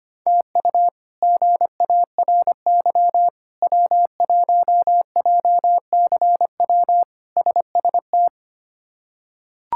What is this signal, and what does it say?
Morse code, heard as a keyed tone, about 25 wpm, 700 Hz. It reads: TU GARY W1JCW SST
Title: K1USN SST QSO Practice - 25wpm